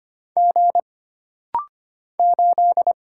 Morse code, heard as a keyed tone, about 25 wpm, 700 Hz. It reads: G 8
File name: Single Character (Letter:Number:Punctuation:Prosign) - Rapid-Fire - 25wpm